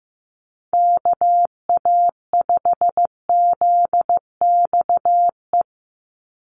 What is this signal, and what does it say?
Morse code, heard as a keyed tone, about 15 wpm, 700 Hz. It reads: KA5ZXE
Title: USA Call Signs 15wpm